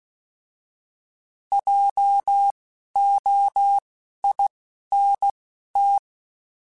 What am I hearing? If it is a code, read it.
Morse code, heard as a keyed tone, about 16 wpm, spaced as for 11 wpm, 800 Hz. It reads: JOINT
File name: Alphabetical / Alphabetical mp3s / Js @10